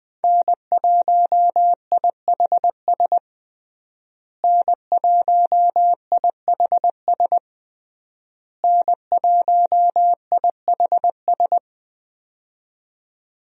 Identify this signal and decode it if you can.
Morse code, heard as a keyed tone, about 20 wpm, 700 Hz. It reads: N1IHS N1IHS N1IHS
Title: USA Call Signs - 5 Characters - Repeated 3x - 20wpm